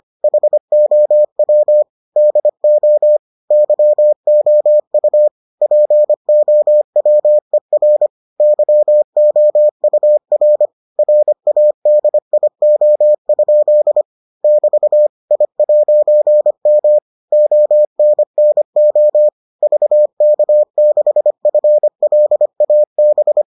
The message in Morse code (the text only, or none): HOW DO YOU POWER YOUR RADIO? <BT> I'M ONNO VK6FLAB